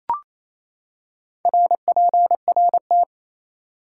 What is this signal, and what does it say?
Morse code, heard as a keyed tone, about 28 wpm, 700 Hz. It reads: RPRT
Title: Top 180 QSO Elements 28wpm